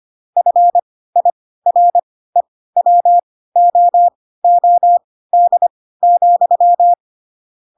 Morse code, heard as a keyed tone, about 25 wpm, spaced as for 16 wpm, 700 Hz. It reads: FIREWOOD,
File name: Pinocchio / Pinocchio ch1 10 25 ik1hge